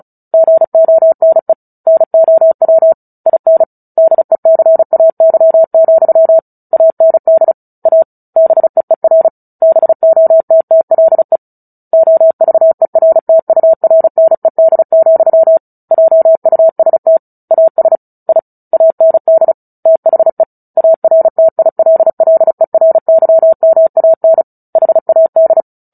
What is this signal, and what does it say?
Morse code, heard as a keyed tone, about 34 wpm, 650 Hz. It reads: GONE NOW IN DECAY, AND A BEER BOTTLE OVERTURNED, JUST AS I AND THE ARTILLERYMAN HAD
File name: Book Two / 09-Chapter 9 - 35 WPM